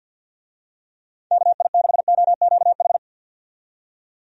Morse code, heard as a keyed tone, about 53 wpm, 700 Hz. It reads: KI6OQH
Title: USA Call Signs - 6 Characters - 50wpm